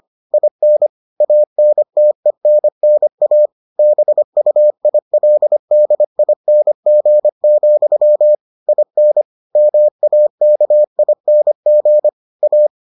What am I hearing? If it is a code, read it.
Morse code, heard as a keyed tone, about 25 wpm, 600 Hz. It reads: IN ANTENNA BUILDING, IN MAKING A